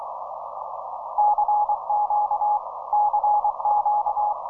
Morse code, heard as a keyed tone, about 24 wpm, 830 Hz, with noise. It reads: CQ CQ